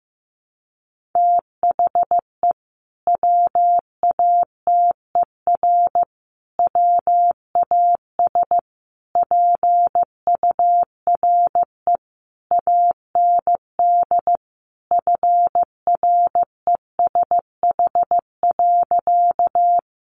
Morse code, heard as a keyed tone, about 15 wpm, 700 Hz. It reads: THE WATER WAS PURE AND FRESH.
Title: Sentences from Top 1500 Words 15wpm